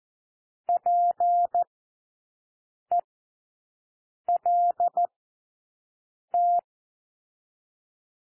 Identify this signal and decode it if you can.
Morse code, heard as a keyed tone, about 14 wpm, spaced as for 6 wpm, 700 Hz. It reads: PELT
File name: cwmp3 / W1 SOUND WORDS 14 wpm 0